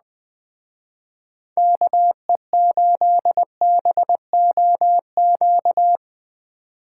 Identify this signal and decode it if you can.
Morse code, heard as a keyed tone, about 20 wpm, 700 Hz. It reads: KE8BOQ
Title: Speed Racing - USA 2x3 Call Signs 17wpm